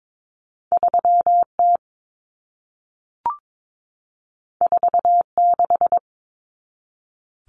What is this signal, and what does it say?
Morse code, heard as a keyed tone, about 22 wpm, 700 Hz. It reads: 3T 46
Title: Common Cut Numbers - 2-Digit 22wpm